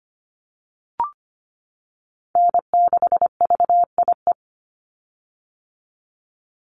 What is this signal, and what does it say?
Morse code, heard as a keyed tone, about 25 wpm, 700 Hz. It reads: N6VIE
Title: USA Call Signs 25wpm